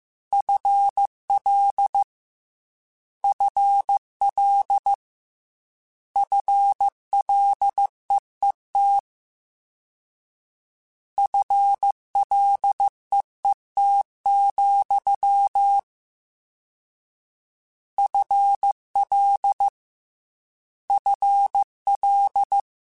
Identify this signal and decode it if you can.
Morse code, heard as a keyed tone, about 15 wpm, 800 Hz. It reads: FL FL FLEET FLEET, FL FL
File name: Combinations / Combinations-mp3s / Combinations@13